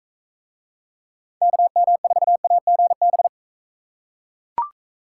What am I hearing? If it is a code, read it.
Morse code, heard as a keyed tone, about 42 wpm, 700 Hz. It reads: KM4AGB